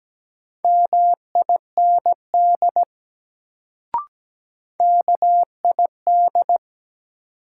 Morse code, heard as a keyed tone, about 17 wpm, 700 Hz. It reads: MIND KID
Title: Sets of 5 Words - New Word Drill - 17wpm